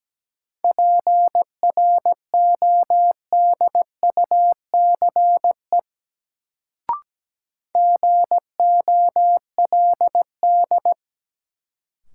Morse code, heard as a keyed tone, about 17 wpm, 700 Hz. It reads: PRODUCE GOLD
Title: Sentences from Top 1000 Words - Review Part 3 - New Word Drill - 17wpm